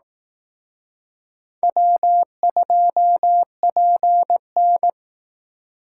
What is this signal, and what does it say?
Morse code, heard as a keyed tone, about 18 wpm, 700 Hz. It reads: W2PN
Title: Speed Racing - USA 1x2 Call Signs 15wpm